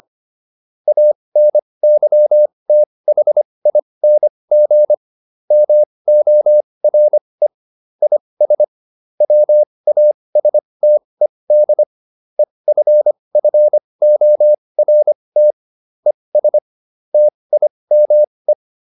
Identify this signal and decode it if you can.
Morse code, heard as a keyed tone, about 25 wpm, spaced as for 20 wpm, 600 Hz. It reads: ANYTHING MORE IS WASTED EFFORT ES TIME